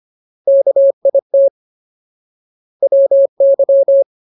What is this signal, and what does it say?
Morse code, heard as a keyed tone, about 25 wpm, 550 Hz. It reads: KIT WY
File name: Ultimate Word List / 25WPM Ultimate